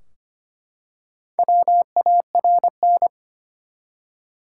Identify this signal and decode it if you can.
Morse code, heard as a keyed tone, about 25 wpm, 700 Hz. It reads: WARN
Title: Four Letter Words - 25wpm